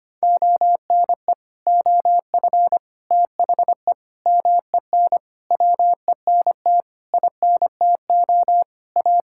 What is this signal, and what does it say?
Morse code, heard as a keyed tone, about 25 wpm, 700 Hz. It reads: ONE OF THE MEN WENT INTO A
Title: Sentences from Top 1600 Words 25wpm